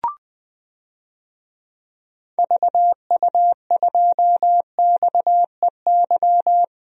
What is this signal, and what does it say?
Morse code, heard as a keyed tone, about 20 wpm, 700 Hz. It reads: VU2XEY
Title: DXCC-100 Call Signs - 6 Characters - Repeated 2x - 20wpm